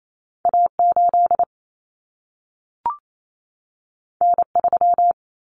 Morse code, heard as a keyed tone, about 28 wpm, 700 Hz. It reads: A8 N3
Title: Common Cut Numbers - 2-Digit 28wpm